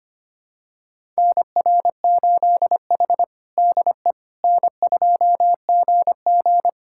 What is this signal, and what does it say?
Morse code, heard as a keyed tone, about 25 wpm, 700 Hz. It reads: NR8H DE N2GG
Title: USA Callsign DE Callsign - 4 Characters - 25wpm